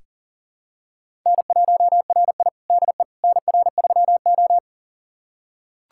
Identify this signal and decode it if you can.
Morse code, heard as a keyed tone, about 40 wpm, 700 Hz. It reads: N1RI DE NR3O
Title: USA Callsign DE Callsign - 4 Characters - 40wpm